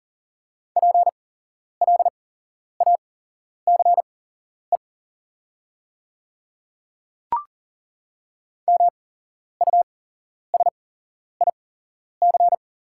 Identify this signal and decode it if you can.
Morse code, heard as a keyed tone, about 40 wpm, spaced as for 11 wpm, 690 Hz. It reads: PLACE MUSIC
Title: 8x Character Spacing - Top 300 Words - 40wpm